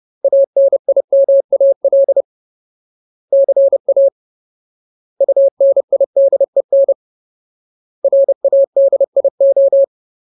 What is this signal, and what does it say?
Morse code, heard as a keyed tone, about 30 wpm, 550 Hz. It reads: ANIMAL CA UNIDEN RADIO
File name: Ultimate Word List / 30WPM Ultimate